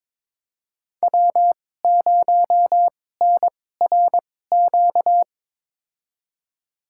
Morse code, heard as a keyed tone, about 22 wpm, spaced as for 16 wpm, 700 Hz. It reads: W0NRQ
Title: USA Call Signs - 5 Characters - 2x Character Spacing - 22wpm